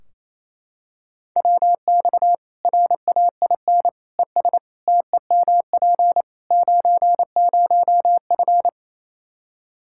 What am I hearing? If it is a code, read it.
Morse code, heard as a keyed tone, about 28 wpm, 700 Hz. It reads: WX RAIN ES TEMP 90F
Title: Practice QSO - Exchange 2 - Info - 28wpm